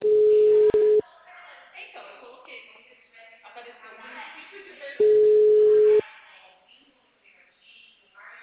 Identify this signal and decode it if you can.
Ringback tone (Europe, 425 Hz), with speech.